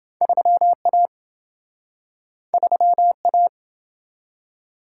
Morse code, heard as a keyed tone, about 28 wpm, 700 Hz. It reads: 3A 3A